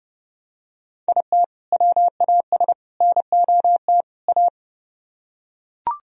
Morse code, heard as a keyed tone, about 28 wpm, 700 Hz. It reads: IT WAS NOT A